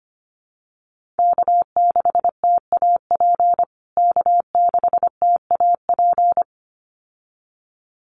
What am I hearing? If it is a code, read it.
Morse code, heard as a keyed tone, about 25 wpm, 700 Hz. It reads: K6TAP K6TAP
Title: USA Call Signs Repeated 2x 25wpm